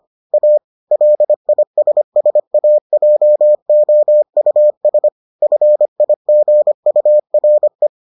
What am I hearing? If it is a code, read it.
Morse code, heard as a keyed tone, about 25 wpm, 600 Hz. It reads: A LISSAJOUS FIGURE